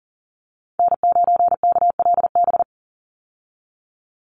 Morse code, heard as a keyed tone, about 40 wpm, 700 Hz. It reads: N9KLB
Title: USA Call Signs 40wpm